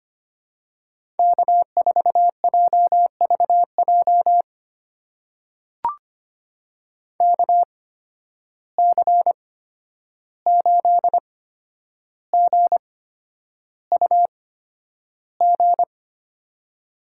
Morse code, heard as a keyed tone, about 25 wpm, spaced as for 10 wpm, 700 Hz. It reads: K4JVJ KC8GUG
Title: USA Call Signs - 8x Character Spacing - 25wpm